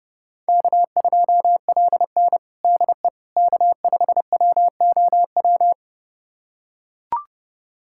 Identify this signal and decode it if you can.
Morse code, heard as a keyed tone, about 30 wpm, 700 Hz. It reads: K2LN DE K5WOW
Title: USA Callsign DE Callsign 30wpm